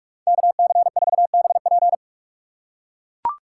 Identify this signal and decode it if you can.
Morse code, heard as a keyed tone, about 44 wpm, 700 Hz. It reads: KC4BP